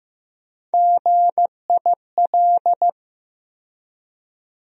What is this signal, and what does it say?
Morse code, heard as a keyed tone, about 15 wpm, 700 Hz. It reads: GIL
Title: USA Call Sign Postfixes - 3 Letters 15wpm